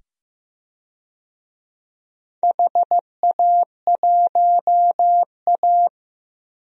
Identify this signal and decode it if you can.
Morse code, heard as a keyed tone, about 15 wpm, 700 Hz. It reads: HA1A
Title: DXCC-100 Call Signs - 4 Characters - 15wpm